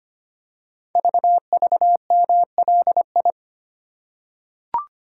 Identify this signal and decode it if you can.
Morse code, heard as a keyed tone, about 25 wpm, 700 Hz. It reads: VVMLI
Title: Code Groups - 5 Letters - 25wpm